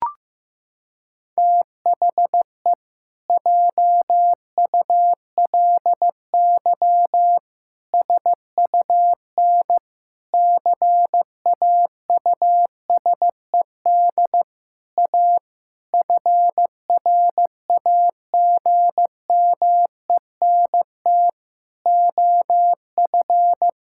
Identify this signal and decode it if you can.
Morse code, heard as a keyed tone, about 15 wpm, 700 Hz. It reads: THE JULY SUN CAUSED A FRAGMENT OF
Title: Pangram Challenge v2 - 15wpm